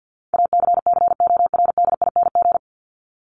Morse code, heard as a keyed tone, about 50 wpm, 700 Hz. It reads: ACCORDING
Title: LY Suffix - Top 100 Words with Base Words 50wpm